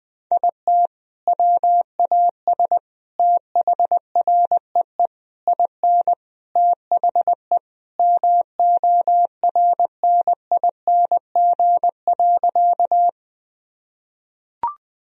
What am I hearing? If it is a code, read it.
Morse code, heard as a keyed tone, about 21 wpm, 700 Hz. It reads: IT WAS THREE IN THE MORNING.
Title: Sentences from Top 1000 Words - Review Part 1 - 20wpm